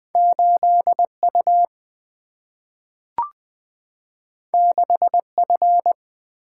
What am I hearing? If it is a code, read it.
Morse code, heard as a keyed tone, about 20 wpm, 700 Hz. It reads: 8U 6F
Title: Single-Letter Single-Number 20wpm